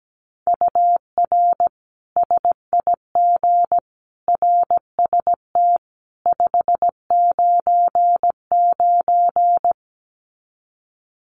Morse code, heard as a keyed tone, about 17 wpm, 700 Hz. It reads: UR SIG RST 599
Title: Common QSO Phrases 17wpm